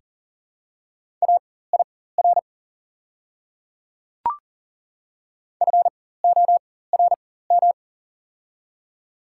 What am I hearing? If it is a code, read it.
Morse code, heard as a keyed tone, about 40 wpm, spaced as for 19 wpm, 700 Hz. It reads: AIR FORM
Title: 4x Character Spacing - Top 500 Words - 40wpm